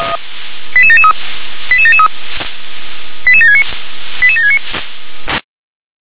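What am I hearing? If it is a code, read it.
Five-tone selective call (ZVEI), heard as quick runs of five tones, tones of about 70 ms. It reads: 88773 88773 88769 88769
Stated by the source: ZVEI five-tone call received by SDR.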